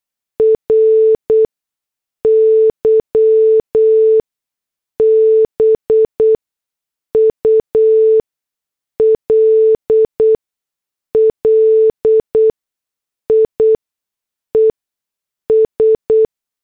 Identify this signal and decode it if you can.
Morse code, heard as a keyed tone, about 8 wpm, spaced as for 6 wpm, 440 Hz. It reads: RY BULLIES
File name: morse / headlines